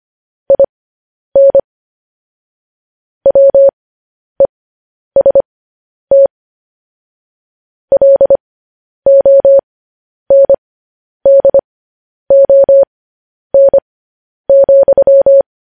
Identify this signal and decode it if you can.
Morse code, heard as a keyed tone, about 25 wpm, spaced as for 10 wpm, 550 Hz. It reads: IN WEST LONDON,
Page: News Headlines in Morse Code at 10 WPM